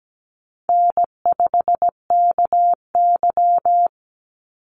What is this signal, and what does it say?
Morse code, heard as a keyed tone, about 17 wpm, 700 Hz. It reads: N5KY
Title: Easy Call Signs Repeated 3x - 17wpm